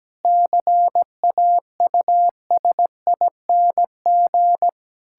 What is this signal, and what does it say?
Morse code, heard as a keyed tone, about 17 wpm, 700 Hz. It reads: CAUSING
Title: Pangram Challenge v2 - 17wpm